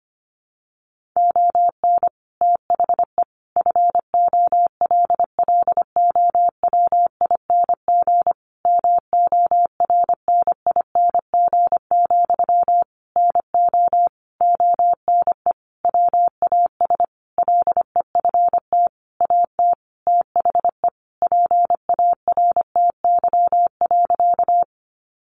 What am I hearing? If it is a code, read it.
Morse code, heard as a keyed tone, about 25 wpm, 700 Hz. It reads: ON THE FOLLOWING MORNING, NO ONE WAS LEFT AT THE PARTY.
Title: Sentences from Top 1400 Words 25wpm